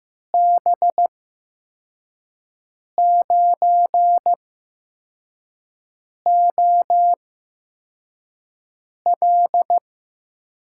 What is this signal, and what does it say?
Morse code, heard as a keyed tone, about 15 wpm, spaced as for 4 wpm, 700 Hz. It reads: B9OL
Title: USA Call Signs - 8x Character Spacing - 15wpm